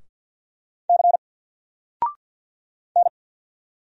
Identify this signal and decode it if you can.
Morse code, heard as a keyed tone, about 48 wpm, 700 Hz. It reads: C N
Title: Single Letter-Number Alt 50wpm